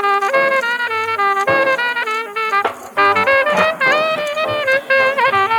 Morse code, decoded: none